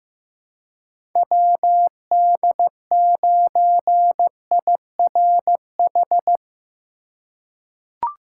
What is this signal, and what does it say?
Morse code, heard as a keyed tone, about 15 wpm, 700 Hz. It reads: WD9IRH
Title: USA Call Signs - 3x Character Spacing - 15wpm